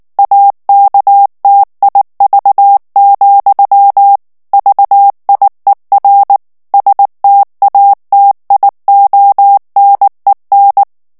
Morse code, heard as a keyed tone, about 19 wpm, 800 Hz. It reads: AKTIV, VIER STATIONEN
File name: morse mp3 / ALLEMAND texte clair 2 vitesse 18 mots minute